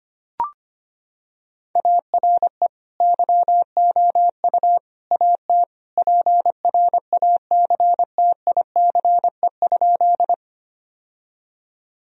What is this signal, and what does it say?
Morse code, heard as a keyed tone, about 25 wpm, 700 Hz. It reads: ARE YOU AT PRACTICE?
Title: Sentences from Top 1000 Words - Review Part 4 - 25wpm